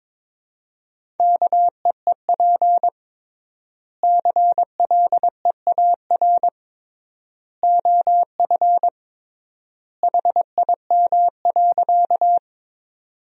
Morse code, heard as a keyed tone, about 22 wpm, 700 Hz. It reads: KEEP CLEAR OF HIM.
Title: 3x Word-Spacing - Sentences from Top 500 Words - 22wpm